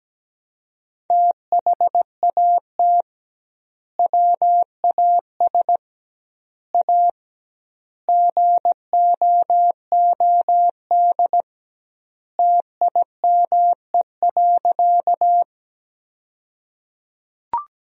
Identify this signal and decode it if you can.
Morse code, heard as a keyed tone, about 17 wpm, 700 Hz. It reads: THAT WAS A GOOD TIME.
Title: 2x Word-Spacing - Sentences from Top 200 Words - 17wpm